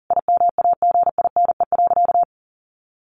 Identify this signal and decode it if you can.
Morse code, heard as a keyed tone, about 37 wpm, 700 Hz. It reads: IMAGINE.
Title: Sentences from Top 1500 Words 40wpm